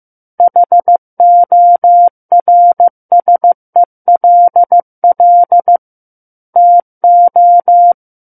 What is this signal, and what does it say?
Morse code, heard as a keyed tone, about 15 wpm, 700 Hz. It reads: HORSELL TO
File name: Book One / 06-Chapter 6 - 15 WPM